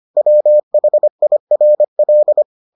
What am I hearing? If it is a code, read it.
Morse code, heard as a keyed tone, about 25 wpm, 600 Hz. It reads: WHIRL